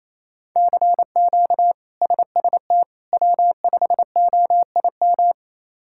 Morse code, heard as a keyed tone, about 27 wpm, 700 Hz. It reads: CQ SST W5OIM
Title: K1USN SST QSO Practice - 28wpm